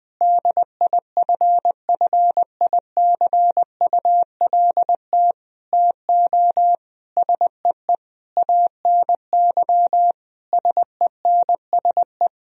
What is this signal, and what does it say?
Morse code, heard as a keyed tone, about 19 wpm, 700 Hz. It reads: DIFFICULT TO SEE ANY SENSE
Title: Sentences from Top 1000 Words - Review Part 4 - 20wpm